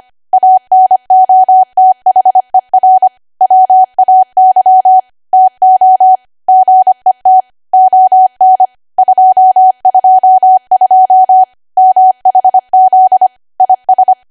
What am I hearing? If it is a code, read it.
Morse code, heard as a keyed tone, about 25 wpm, 750 Hz. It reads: ANOTHER WAY TO GET ON 222 MHZ IS
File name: Archive / 25 WPM / 25WPM